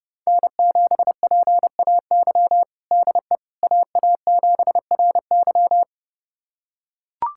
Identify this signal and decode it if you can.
Morse code, heard as a keyed tone, about 30 wpm, 700 Hz. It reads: N7PAY DE AA7RY
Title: USA Callsign DE Callsign - 5 Characters - 30wpm